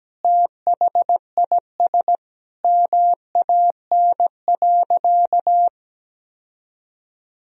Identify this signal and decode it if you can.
Morse code, heard as a keyed tone, about 17 wpm, 700 Hz. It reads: THIS MAN.
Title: Sentences from Top 1000 Words - Review Part 1 - 17wpm